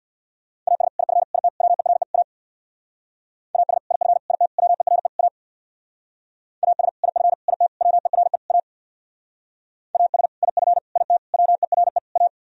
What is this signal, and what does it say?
Morse code, heard as a keyed tone, about 63 wpm, 700 Hz. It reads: AS IF IT WERE A AS IF IT WERE A AS IF IT WERE A AS IF IT WERE A